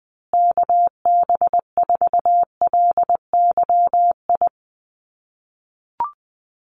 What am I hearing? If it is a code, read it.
Morse code, heard as a keyed tone, about 20 wpm, 700 Hz. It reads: KB4LYI
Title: USA Call Signs Repeated 2x 20wpm